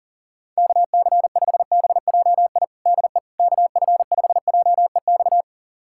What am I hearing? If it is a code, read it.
Morse code, heard as a keyed tone, about 40 wpm, 700 Hz. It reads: KC5BJI DE KF5JEX
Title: USA Callsign DE Callsign - 6 Characters - 40wpm